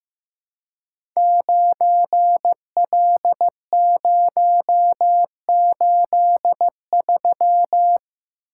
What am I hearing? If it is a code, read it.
Morse code, heard as a keyed tone, about 15 wpm, 700 Hz. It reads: 9L083
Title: Code Groups - 5 Letter:Numbers - 15wpm